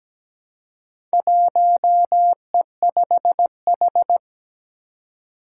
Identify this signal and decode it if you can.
Morse code, heard as a keyed tone, about 17 wpm, 700 Hz. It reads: 1E5H
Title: Four Challenging Letter-Number Combinations 17wpm